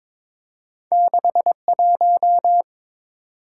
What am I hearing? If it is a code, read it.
Morse code, heard as a keyed tone, about 22 wpm, 690 Hz. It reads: 61
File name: practice / 22wpm / Two Letter-Number Combinations - 22wpm